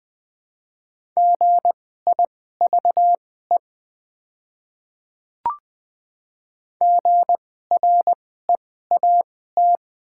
Morse code, heard as a keyed tone, about 20 wpm, spaced as for 14 wpm, 700 Hz. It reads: GIVE GREAT
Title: 2x Character Spacing - Top 500 Words - 20wpm